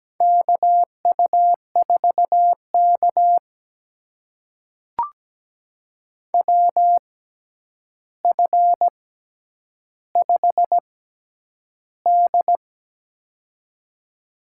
Morse code, heard as a keyed tone, about 17 wpm, spaced as for 9 wpm, 700 Hz. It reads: KU4K WF5D
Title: USA Call Signs - 4 Characters - 6x Character Spacing - 17wpm